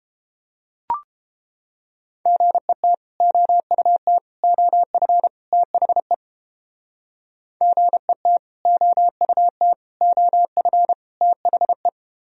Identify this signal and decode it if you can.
Morse code, heard as a keyed tone, about 29 wpm, 700 Hz. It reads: GET OUT OF THE GET OUT OF THE